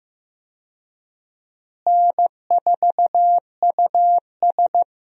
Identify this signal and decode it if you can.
Morse code, heard as a keyed tone, about 15 wpm, 700 Hz. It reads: N4US
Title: USA Call Signs - 4 Characters - Repeated 3x - 15wpm